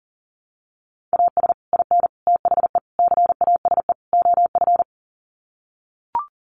Morse code, heard as a keyed tone, about 40 wpm, 700 Hz. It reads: AS IN THE CASE OF
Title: Sets of 5 Words 40wpm